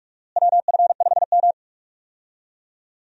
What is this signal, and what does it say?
Morse code, heard as a keyed tone, about 41 wpm, 700 Hz. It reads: WF5M
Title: USA Call Signs - 4 Characters - Repeated 3x - 45wpm